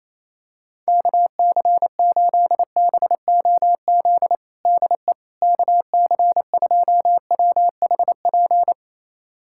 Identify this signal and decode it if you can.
Morse code, heard as a keyed tone, about 28 wpm, 700 Hz. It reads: KC8BOZ DE KC2WHP
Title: USA Callsign DE Callsign - 6 Characters - 28wpm